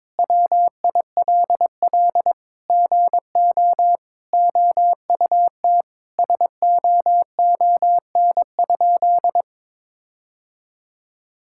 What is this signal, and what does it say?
Morse code, heard as a keyed tone, about 22 wpm, 700 Hz. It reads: WILL GO OUT SOON?
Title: Sentences from Top 1600 Words 22wpm